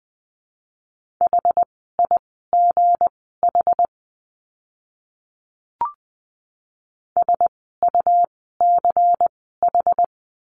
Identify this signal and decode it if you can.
Morse code, heard as a keyed tone, about 20 wpm, spaced as for 15 wpm, 700 Hz. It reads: HIGH SUCH
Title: 237 Lesson 38f - Letter q - Words 20wpm